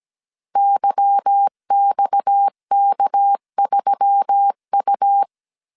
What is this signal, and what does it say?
Morse code, heard as a keyed tone, about 17 wpm, 800 Hz. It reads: YXK3U
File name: CW / DET / 1 9 1500C S C 16wpm 2909